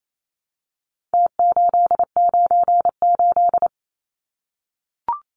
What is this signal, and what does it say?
Morse code, heard as a keyed tone, about 28 wpm, 700 Hz. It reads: T898
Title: Numbers - Contest Style - 4-Digit 28wpm